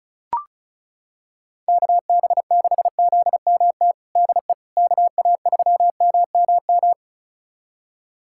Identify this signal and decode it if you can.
Morse code, heard as a keyed tone, about 36 wpm, 700 Hz. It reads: KB6ZMT DE KA3MMM
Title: USA Callsign DE Callsign 35wpm